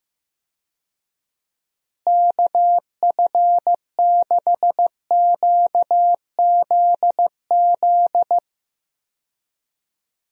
Morse code, heard as a keyed tone, about 15 wpm, 700 Hz. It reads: KF6QZZ
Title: USA Call Signs - 6 Characters - Repeated 2x - 15wpm